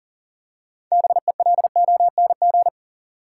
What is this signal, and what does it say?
Morse code, heard as a keyed tone, about 40 wpm, 700 Hz. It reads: BELONG
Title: Six Letter Words - 40wpm